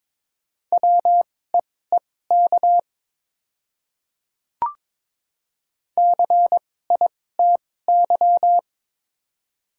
Morse code, heard as a keyed tone, about 22 wpm, spaced as for 16 wpm, 700 Hz. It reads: WEEK CITY
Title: 2x Character Spacing - Top 500 Words - 22wpm